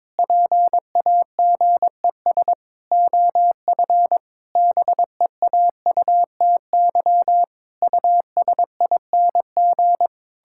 Morse code, heard as a keyed tone, about 22 wpm, 700 Hz. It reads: PAGES OF BEAUTY USING
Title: Pangram Challenge v2 - 22wpm